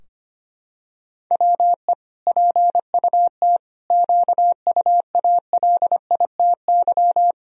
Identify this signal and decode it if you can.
Morse code, heard as a keyed tone, about 25 wpm, 700 Hz. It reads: WE PUT QUALITY
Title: Sentences from Top 1000 Words - Review Part 4 - 25wpm